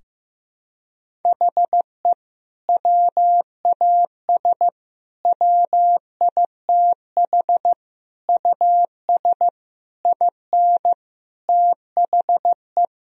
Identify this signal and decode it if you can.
Morse code, heard as a keyed tone, about 15 wpm, 700 Hz. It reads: HE WAS WITH US IN THE
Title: Sentences from Top 1000 Words - Review Part 3 - 15wpm